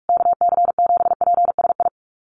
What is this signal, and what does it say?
Morse code, heard as a keyed tone, about 44 wpm, 700 Hz. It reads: KC7PSI
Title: USA Call Signs Repeated 3x 45wpm